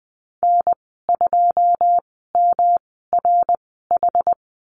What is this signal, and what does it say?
Morse code, heard as a keyed tone, about 20 wpm, spaced as for 14 wpm, 700 Hz. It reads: N2MRH
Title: 147 Lesson 24g - Letter y - Call Signs 20wpm